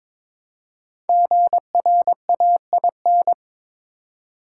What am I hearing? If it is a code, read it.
Morse code, heard as a keyed tone, about 22 wpm, 700 Hz. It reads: GRAIN
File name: 5 Letter Words - First 3 Letters Given - 22wpm